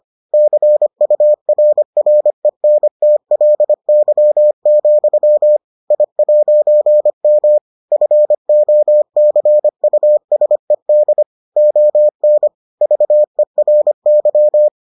Morse code, heard as a keyed tone, about 25 wpm, 600 Hz. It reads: CURRENTLY, I'M FOCUSED ON VERY